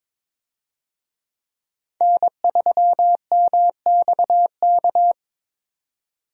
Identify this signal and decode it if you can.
Morse code, heard as a keyed tone, about 22 wpm, 700 Hz. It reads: N3MXK
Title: USA Call Signs - 5 Characters - 22wpm